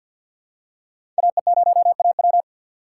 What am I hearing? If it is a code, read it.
Morse code, heard as a keyed tone, about 49 wpm, 700 Hz. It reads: AE0AW
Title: USA Call Signs - 5 Characters - 2x Character Spacing - 50wpm